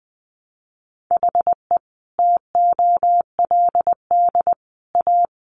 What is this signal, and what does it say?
Morse code, heard as a keyed tone, about 20 wpm, 700 Hz. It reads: HE TOLD A
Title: Sentences from Top 1300 Words 20wpm